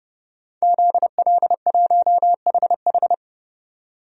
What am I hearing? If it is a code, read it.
Morse code, heard as a keyed tone, about 30 wpm, 700 Hz. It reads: ZL1HH
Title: DXCC-100 Call Signs - 5 Characters - 30wpm